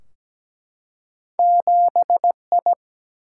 Morse code, heard as a keyed tone, about 17 wpm, 700 Hz. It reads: 7I
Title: Single-Letter Single-Number 17wpm